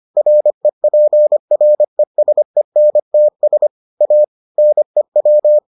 Morse code, heard as a keyed tone, about 25 wpm, 600 Hz. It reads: REPRESENTS A NEW